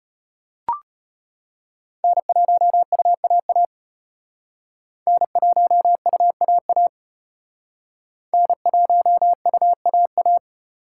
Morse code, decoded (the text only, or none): N1UAA N1UAA N1UAA